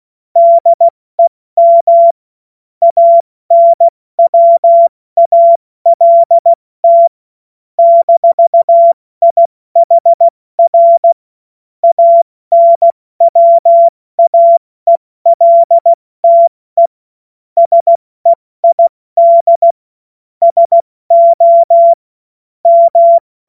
Morse code, heard as a keyed tone, about 16 wpm, 680 Hz. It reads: DEM ANWALT -IHR ANWAELTE SEID SO M
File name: Morsekurs / Witze / Witz 17. Mai 2022 70 F80